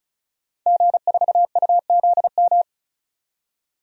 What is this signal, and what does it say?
Morse code, heard as a keyed tone, about 34 wpm, 700 Hz. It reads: G4UZM
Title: DXCC-100 Call Signs - 5 Characters - 35wpm